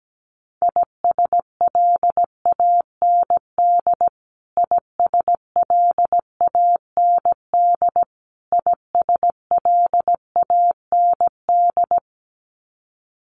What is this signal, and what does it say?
Morse code, heard as a keyed tone, about 17 wpm, 700 Hz. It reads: ISLAND ISLAND ISLAND
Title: Top 500 Words Repeated 17wpm